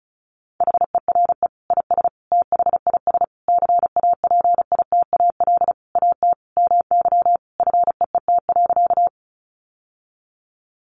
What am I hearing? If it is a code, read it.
Morse code, heard as a keyed tone, about 34 wpm, 700 Hz. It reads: HERE IS THIS CAPITAL AT MY FEET.
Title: Sentences from Top 1000 Words 35wpm